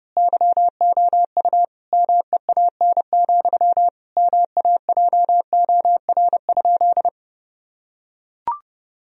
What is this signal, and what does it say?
Morse code, heard as a keyed tone, about 32 wpm, 700 Hz. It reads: YOU MEAN, MAJOR?
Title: Sentences from Top 1000 Words - Review Part 4 - 30wpm